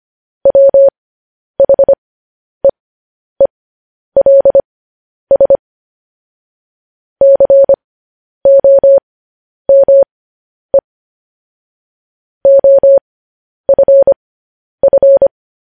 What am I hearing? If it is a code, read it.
Morse code, heard as a keyed tone, about 25 wpm, spaced as for 10 wpm, 550 Hz. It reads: WHEELS COME OFF
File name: podcast / News headlines-1x-25x10